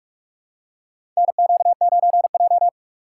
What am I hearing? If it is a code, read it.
Morse code, heard as a keyed tone, about 44 wpm, 700 Hz. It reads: NQ9J
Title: USA Call Signs - 4 Characters - 3x Character Spacing - 45wpm